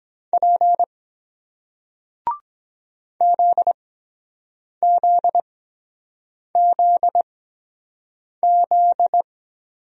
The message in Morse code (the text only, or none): P ZZZZ